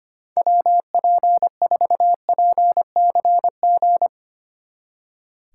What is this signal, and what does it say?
Morse code, heard as a keyed tone, about 31 wpm, 700 Hz. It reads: WP4PCG
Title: Speed Racing - USA 2x3 Call Signs 25wpm